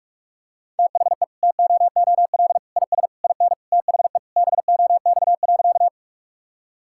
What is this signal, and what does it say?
Morse code, heard as a keyed tone, about 46 wpm, 700 Hz. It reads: THE TOOL IS IN THE BOX.
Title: Sentences from Top 1000 Words - Review Part 6 - 45wpm